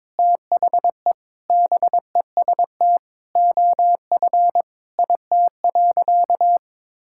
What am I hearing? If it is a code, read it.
Morse code, heard as a keyed tone, about 21 wpm, 700 Hz. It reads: THE BEST OF IT.
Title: Sentences from Top 1000 Words - Review Part 5 - 22wpm